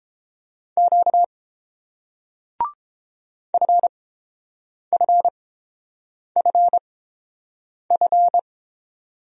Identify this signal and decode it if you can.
Morse code, heard as a keyed tone, about 30 wpm, spaced as for 7 wpm, 700 Hz. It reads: Q FFFF